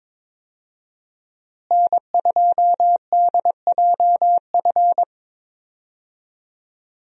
Morse code, heard as a keyed tone, about 22 wpm, 700 Hz. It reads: N2DJF
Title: USA Call Signs - 5 Characters - Repeated 3x - 22wpm